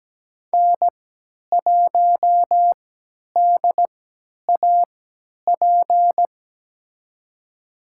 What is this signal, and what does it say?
Morse code, heard as a keyed tone, about 17 wpm, spaced as for 10 wpm, 700 Hz. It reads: N1DAP
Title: USA Call Signs - 5 Characters - 3x Character Spacing - 17wpm